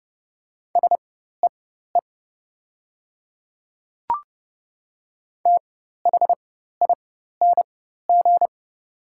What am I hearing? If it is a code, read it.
Morse code, heard as a keyed tone, about 30 wpm, spaced as for 14 wpm, 700 Hz. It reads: SEE THING
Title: 4x Character Spacing - Top 100 Words - 30wpm